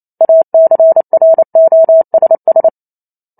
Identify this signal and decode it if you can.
Morse code, heard as a keyed tone, about 28 wpm, 650 Hz. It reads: ACROSS